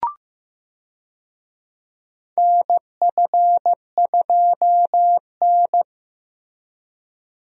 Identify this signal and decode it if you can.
Morse code, heard as a keyed tone, about 15 wpm, 700 Hz. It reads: NF2N
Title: USA Call Signs - 4 Characters - Repeated 3x - 15wpm